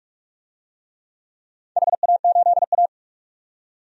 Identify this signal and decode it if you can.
Morse code, heard as a keyed tone, about 44 wpm, 700 Hz. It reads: HA8A
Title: DXCC-100 Call Signs - 4 Characters - 45wpm